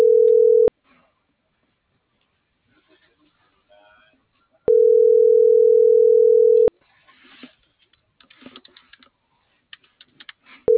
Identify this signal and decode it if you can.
Ringback tone (North America), 440 + 480 Hz.